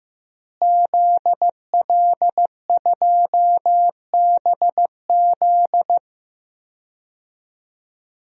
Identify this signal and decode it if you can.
Morse code, heard as a keyed tone, about 15 wpm, 700 Hz. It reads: ZL2BZ
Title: DXCC-100 Call Signs - 5 Characters - Repeated 3x - 15wpm